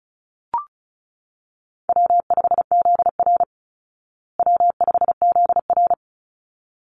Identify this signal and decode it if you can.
Morse code, heard as a keyed tone, about 34 wpm, 700 Hz. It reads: W5ZR W5ZR
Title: Easy Call Signs Repeated 2x - 35wpm